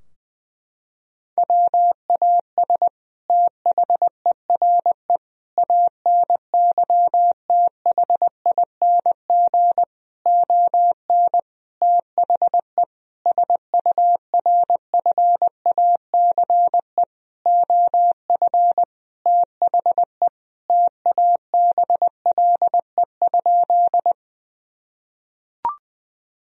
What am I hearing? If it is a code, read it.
Morse code, heard as a keyed tone, about 21 wpm, 700 Hz. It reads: WAS THERE ANYTHING ON THE SURFACE OF THE TABLE?
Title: Sentences from Top 1000 Words - Review Part 3 - 20wpm